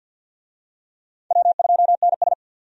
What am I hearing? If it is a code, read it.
Morse code, heard as a keyed tone, about 52 wpm, 700 Hz. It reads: W2NS
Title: USA Call Signs - 4 Characters - 50wpm